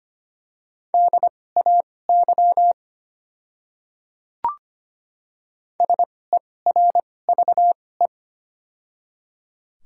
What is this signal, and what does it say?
Morse code, heard as a keyed tone, about 25 wpm, spaced as for 18 wpm, 700 Hz. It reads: DAY SERVE
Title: 2x Character Spacing - Top 500 Words - 25wpm